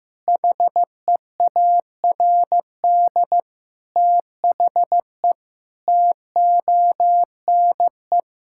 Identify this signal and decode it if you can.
Morse code, heard as a keyed tone, about 15 wpm, 700 Hz. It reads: HEARD THE TONE
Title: Sentences from Top 1600 Words 15wpm